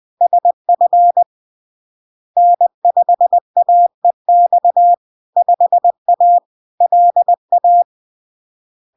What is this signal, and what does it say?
Morse code, heard as a keyed tone, about 20 wpm, 700 Hz. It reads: SF N5AEX 5A LA
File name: LARK-FD-2022-Log-in-CW-20wpm-700-Hz